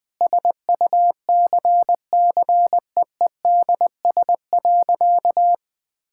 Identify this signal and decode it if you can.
Morse code, heard as a keyed tone, about 20 wpm, 700 Hz. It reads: SUCCEEDS.
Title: Sentences with 8-Word Phrases - 20wpm